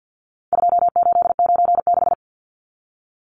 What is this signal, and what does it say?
Morse code, heard as a keyed tone, about 51 wpm, 700 Hz. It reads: 2896
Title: Numbers - 4-Digit 50wpm